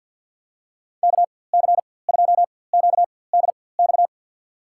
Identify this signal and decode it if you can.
Morse code, heard as a keyed tone, about 51 wpm, spaced as for 23 wpm, 700 Hz. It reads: KC2QDX
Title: USA Call Signs - 4x Character Spacing - 50wpm